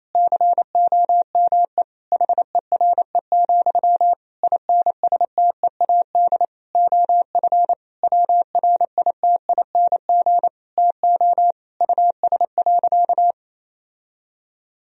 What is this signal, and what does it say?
Morse code, heard as a keyed tone, about 28 wpm, 700 Hz. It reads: COME HERE, INSTEAD OF WRITING TO US.
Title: Sentences from Top 1000 Words - Review Part 4 - 28wpm